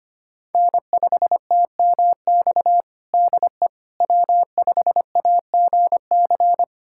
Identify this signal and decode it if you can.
Morse code, heard as a keyed tone, about 25 wpm, 700 Hz. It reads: N5TMX DE W5AGC
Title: USA Callsign DE Callsign - 5 Characters - 25wpm